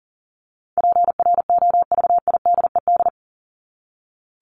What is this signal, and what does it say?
Morse code, heard as a keyed tone, about 40 wpm, 700 Hz. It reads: PROVIDED
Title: ED Suffix - Top 100 Words with Base Words 40wpm